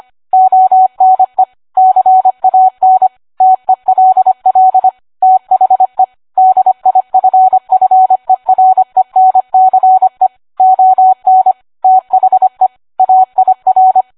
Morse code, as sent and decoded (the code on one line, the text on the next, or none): --- -. / -.-. .- -. / - . .-.. .-.. / - .... / -.. .. ..-. ..-. .-. -. -.-. / --- -. / - .... / .- .. .-.
ONE CAN TELL THE DIFFERENCE ON THE AIR